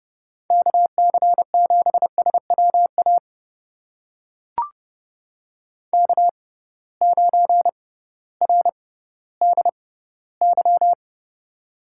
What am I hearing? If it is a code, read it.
Morse code, heard as a keyed tone, about 30 wpm, spaced as for 16 wpm, 700 Hz. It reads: KC7SWA K9RDY
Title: USA Call Signs - 6x Character Spacing - 30wpm